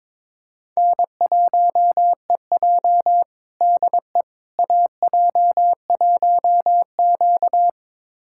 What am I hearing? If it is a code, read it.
Morse code, heard as a keyed tone, about 22 wpm, 700 Hz. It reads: N1EJ DE AJ1Q
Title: USA Callsign DE Callsign - 4 Characters - 22wpm